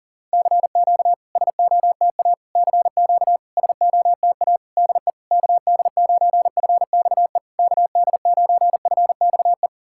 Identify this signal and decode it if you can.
Morse code, heard as a keyed tone, about 40 wpm, 690 Hz. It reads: CQ SOTA CQ SOTA DE KD9FXE KD9FXE
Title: SOTA QSO Practice 40wpm